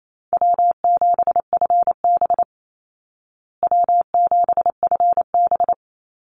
Morse code, heard as a keyed tone, about 28 wpm, 700 Hz. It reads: W7FB W7FB
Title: Easy Call Signs Repeated 2x - 28wpm